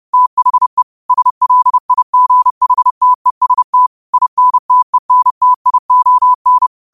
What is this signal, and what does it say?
Morse code, heard as a keyed tone, about 30 wpm, 1000 Hz. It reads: THE SLIGHTEST INTENTION
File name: qotdfiles / 30wpm